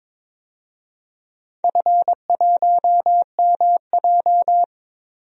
Morse code, heard as a keyed tone, about 21 wpm, 700 Hz. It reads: F1MJ
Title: DXCC-100 Call Signs - 4 Characters - Repeated 2x - 22wpm